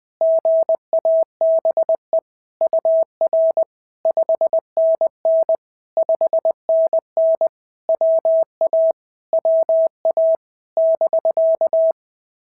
Morse code, heard as a keyed tone, about 19 wpm, 650 Hz. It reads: GABE UR 5NN 5NN WA WA <BK>
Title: POTA QSO Practice - 20wpm